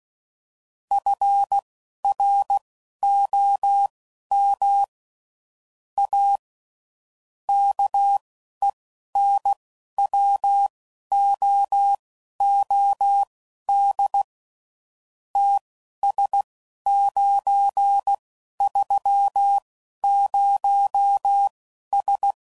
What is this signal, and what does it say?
Morse code, heard as a keyed tone, about 16 wpm, spaced as for 11 wpm, 800 Hz. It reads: FROM A KENWOOD TS930S
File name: CW Code Groups @10WPM / QSOs / QSO kc7haa